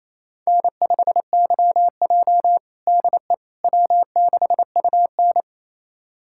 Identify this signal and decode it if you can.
Morse code, heard as a keyed tone, about 28 wpm, 700 Hz. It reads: N5YJ DE W6UN
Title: USA Callsign DE Callsign - 4 Characters - 28wpm